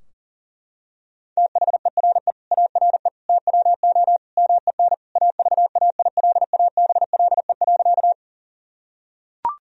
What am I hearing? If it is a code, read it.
Morse code, heard as a keyed tone, about 40 wpm, 700 Hz. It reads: THERE ARE TWO MEN AVAILABLE.
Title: Sentences from Top 1000 Words - Review Part 3 - 40wpm